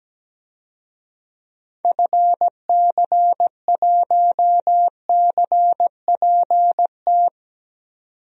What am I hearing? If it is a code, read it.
Morse code, heard as a keyed tone, about 17 wpm, 700 Hz. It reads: FC1CPT
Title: DXCC-100 Call Signs - 6 Characters - 17wpm